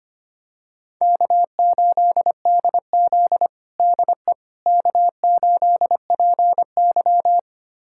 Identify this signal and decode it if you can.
Morse code, heard as a keyed tone, about 25 wpm, 700 Hz. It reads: K8DZ DE K8PY
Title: USA Callsign DE Callsign - 4 Characters - 25wpm